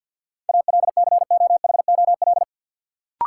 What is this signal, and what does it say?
Morse code, heard as a keyed tone, about 47 wpm, 700 Hz. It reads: ALCOHOL
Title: Sentences from Top 1400 Words - New Word Drill - 50wpm